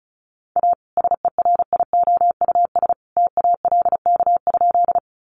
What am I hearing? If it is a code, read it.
Morse code, heard as a keyed tone, about 34 wpm, 700 Hz. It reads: A SERIOUS TALK?
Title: Sentences from Top 1100 Words 35wpm